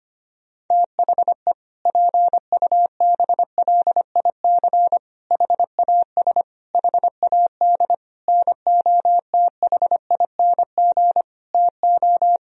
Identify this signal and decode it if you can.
Morse code, heard as a keyed tone, about 25 wpm, 700 Hz. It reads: THE PUBLIC HAS HAD NOTHING TO
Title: Sentences from Top 1000 Words - Review Part 3 - 25wpm